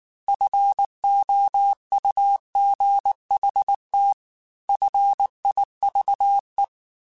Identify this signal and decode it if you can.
Morse code, heard as a keyed tone, about 19 wpm, 800 Hz. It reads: FOUGHT FIVE